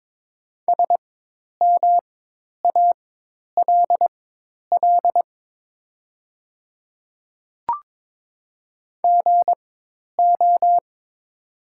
Morse code, heard as a keyed tone, about 22 wpm, spaced as for 10 wpm, 700 Hz. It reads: SMALL GO
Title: 4x Character Spacing - Top 500 Words - 22wpm